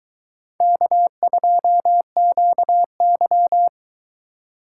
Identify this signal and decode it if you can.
Morse code, heard as a keyed tone, about 21 wpm, 700 Hz. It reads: K2QY